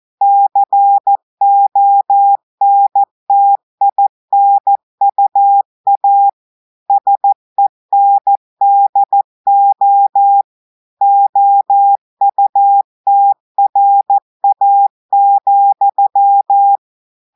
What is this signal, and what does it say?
Morse code, heard as a keyed tone, about 14 wpm, 800 Hz. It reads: CONTINUA SENDO OUTRA,
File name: audios / portugais / portugais texte en clair 1 vitesse 14 mots minutes